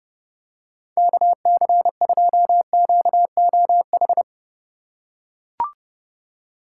Morse code, heard as a keyed tone, about 29 wpm, 700 Hz. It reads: KC2QOH